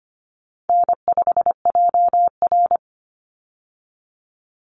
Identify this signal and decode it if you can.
Morse code, heard as a keyed tone, about 25 wpm, 700 Hz. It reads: N5JR
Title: Easy Call Signs Repeated 2x - 25wpm